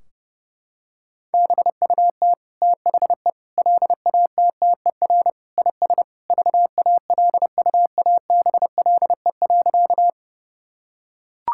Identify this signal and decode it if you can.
Morse code, heard as a keyed tone, about 30 wpm, 700 Hz. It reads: BUT THE LATTER IS VALUABLE.
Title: Sentences from Top 1600 Words 30wpm